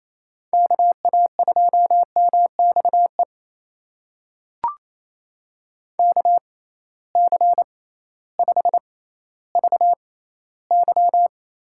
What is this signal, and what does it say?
Morse code, heard as a keyed tone, about 28 wpm, spaced as for 15 wpm, 700 Hz. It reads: KA2MXE KC5VY
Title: USA Call Signs - 6 Characters - 6x Character Spacing - 28wpm